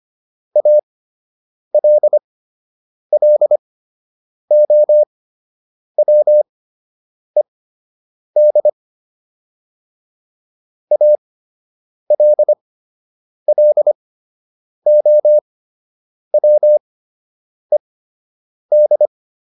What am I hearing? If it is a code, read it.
Morse code, heard as a keyed tone, about 25 wpm, spaced as for 8 wpm, 600 Hz. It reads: ALLOWED ALLOWED